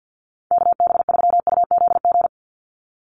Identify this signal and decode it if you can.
Morse code, heard as a keyed tone, about 50 wpm, 700 Hz. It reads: KB3UZG
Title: USA Call Signs 50wpm